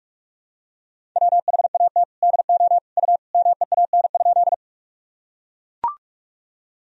Morse code, heard as a keyed tone, about 42 wpm, 700 Hz. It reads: WHAT DO U MEAN?
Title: Sentences from Top 200 Words and Abbreviations 45wpm